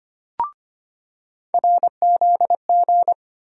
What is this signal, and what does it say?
Morse code, heard as a keyed tone, about 25 wpm, 700 Hz. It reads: RZG
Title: USA Call Sign Postfixes - 3 Letters 25wpm